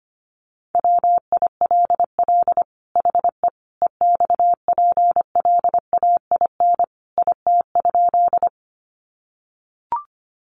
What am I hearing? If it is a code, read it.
Morse code, heard as a keyed tone, about 25 wpm, 700 Hz. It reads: WILL HE EXPLAIN IT?
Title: Sentences from Top 1100 Words 25wpm